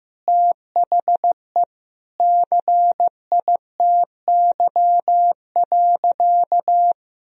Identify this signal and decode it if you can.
Morse code, heard as a keyed tone, about 15 wpm, 700 Hz. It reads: THE CITY.
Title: Sentences from Top 1000 Words - Review Part 2 - 15wpm